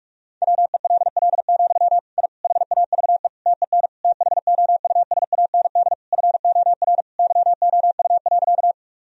Morse code, heard as a keyed tone, about 46 wpm, 700 Hz. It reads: WELL, I HAVE TEN THOUSAND FOR YOU.
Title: Sentences from Top 1000 Words - Review Part 3 - 45wpm